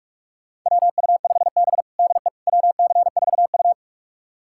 Morse code, heard as a keyed tone, about 46 wpm, 700 Hz. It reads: WU5B DE WC4U
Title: USA Callsign DE Callsign - 4 Characters - 45wpm